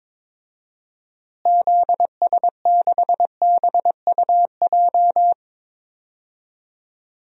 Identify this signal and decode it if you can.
Morse code, heard as a keyed tone, about 22 wpm, 700 Hz. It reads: ZS6BUJ
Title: DXCC-100 Call Signs - 6 Characters - Repeated 3x - 22wpm